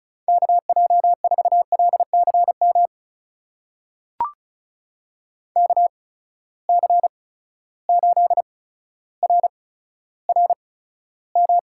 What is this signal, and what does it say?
Morse code, heard as a keyed tone, about 36 wpm, spaced as for 15 wpm, 700 Hz. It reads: KJ4LCM KC8RRM
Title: USA Call Signs - 6 Characters - 8x Character Spacing - 35wpm